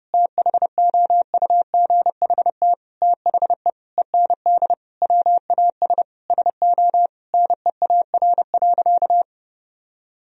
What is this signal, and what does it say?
Morse code, heard as a keyed tone, about 32 wpm, 700 Hz. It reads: THOUGHT THE END WAS SO NEAR.
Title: Sentences from Top 300 Words and Abbreviations 30wpm